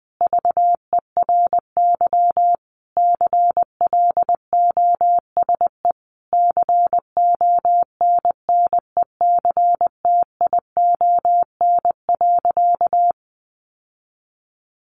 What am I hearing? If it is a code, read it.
Morse code, heard as a keyed tone, about 20 wpm, 700 Hz. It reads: VERY CLOSE CONNECTION.
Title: Sentences from Top 1300 Words 20wpm